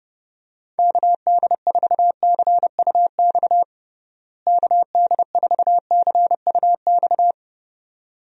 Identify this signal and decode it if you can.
Morse code, heard as a keyed tone, about 30 wpm, 700 Hz. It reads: KD4CUX KD4CUX
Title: USA Call Signs - 6 Characters - Repeated 2x - 30wpm